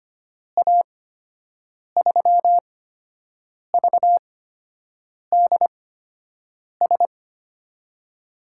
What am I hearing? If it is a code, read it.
Morse code, heard as a keyed tone, about 25 wpm, spaced as for 7 wpm, 700 Hz. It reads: A3VDS
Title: USA Call Signs - 6 Characters - 8x Character Spacing - 25wpm